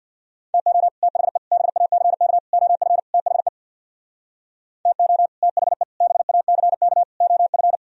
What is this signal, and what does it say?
Morse code, heard as a keyed tone, about 65 wpm, 700 Hz. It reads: TO THE BACK OF THE TO THE BACK OF